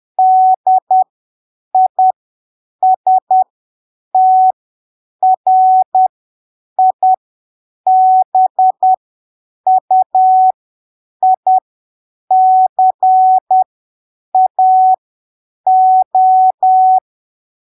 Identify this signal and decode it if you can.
Morse code, heard as a keyed tone, about 10 wpm, spaced as for 7 wpm, 750 Hz. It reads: DISTRIBUICAO
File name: arquivos / texto / texto 07 10ppm